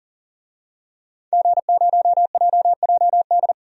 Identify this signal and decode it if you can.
Morse code, heard as a keyed tone, about 39 wpm, 700 Hz. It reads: G0JJD
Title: DXCC-100 Call Signs - 5 Characters - 40wpm